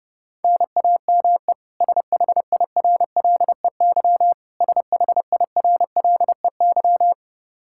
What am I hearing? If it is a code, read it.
Morse code, heard as a keyed tone, about 30 wpm, 700 Hz. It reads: NAME SHIRLEY SHIRLEY
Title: K1USN SST QSO Practice 30wpm